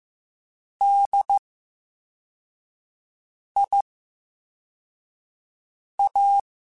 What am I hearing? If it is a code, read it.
Morse code, heard as a keyed tone, about 15 wpm, 800 Hz. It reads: D I A
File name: Special INGs-mp3s / Special INGs-mp3s / VerySlowToFaster Ing @13